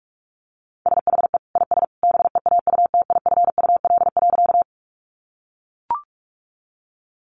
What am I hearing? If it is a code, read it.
Morse code, heard as a keyed tone, about 44 wpm, 700 Hz. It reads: SHE IS BEAUTIFUL.
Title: Sentences from Top 900 Words 45wpm